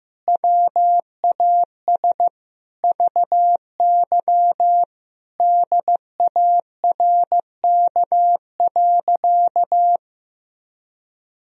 Morse code, heard as a keyed tone, about 15 wpm, 700 Hz. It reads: WAS VY DARK.
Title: Sentences from Top 500 Words and Abbreviations 15wpm